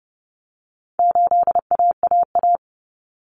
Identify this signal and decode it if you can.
Morse code, heard as a keyed tone, about 30 wpm, 700 Hz. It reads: 8AAA
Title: Common Cut Numbers - 4-Digit 30wpm